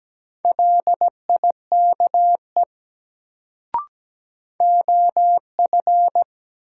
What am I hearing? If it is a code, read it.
Morse code, heard as a keyed tone, about 17 wpm, 700 Hz. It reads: LIKE OF
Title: Sentences from Top 100 Words - New Word Drill - 17wpm